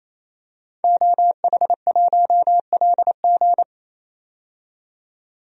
Morse code, heard as a keyed tone, about 27 wpm, 700 Hz. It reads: OH1LG
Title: DXCC-100 Call Signs - 5 Characters - 28wpm